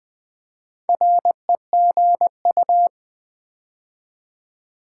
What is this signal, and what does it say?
Morse code, heard as a keyed tone, about 20 wpm, 700 Hz. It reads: REGU
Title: Build A Word - 4 Letters At A Time - 10-Letter Words - 20wpm